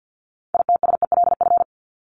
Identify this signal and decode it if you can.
Morse code, heard as a keyed tone, about 50 wpm, 700 Hz. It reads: ITSELF
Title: Top 501-750 Words 50wpm